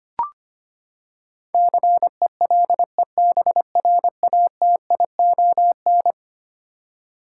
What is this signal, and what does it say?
Morse code, heard as a keyed tone, about 25 wpm, 700 Hz. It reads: CELEBRATION
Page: morse-code-ninja-practice-25wpm